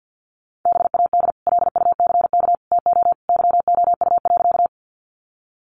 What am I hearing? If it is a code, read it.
Morse code, heard as a keyed tone, about 50 wpm, 700 Hz. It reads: BAD LUCK TO YOU.
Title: Sentences from Top 1500 Words 50wpm